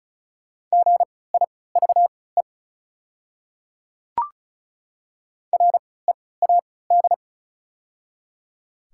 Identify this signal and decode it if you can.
Morse code, heard as a keyed tone, about 36 wpm, spaced as for 20 wpm, 700 Hz. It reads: GIVE READ
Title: 3x Character Spacing - Top 200 Words - 35wpm